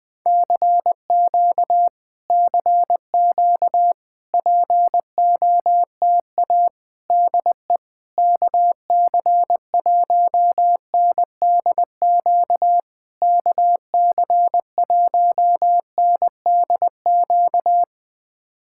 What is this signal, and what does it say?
Morse code, heard as a keyed tone, about 20 wpm, 700 Hz. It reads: CQ CQ POTA DE KC1NDQ KC1NDQ
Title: POTA QSO Practice - 20wpm